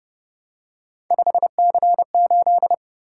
Morse code, heard as a keyed tone, about 30 wpm, 700 Hz. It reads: 5C8